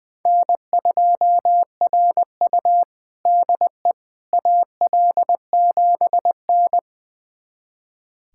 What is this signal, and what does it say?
Morse code, heard as a keyed tone, about 21 wpm, 700 Hz. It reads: N2RU DE AL7N
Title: USA Callsign DE Callsign - 4 Characters - 20wpm